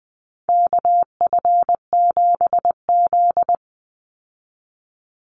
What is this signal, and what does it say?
Morse code, heard as a keyed tone, about 20 wpm, 700 Hz. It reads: KF7Z
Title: Call Signs Repeated 3x - 20wpm